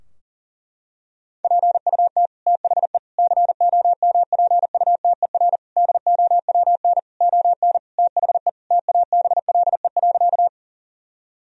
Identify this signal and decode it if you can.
Morse code, heard as a keyed tone, about 40 wpm, 700 Hz. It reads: PUT THE COMPUTER DOWN ON THE TABLE.
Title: Sentences from Top 1000 Words - Review Part 3 - 40wpm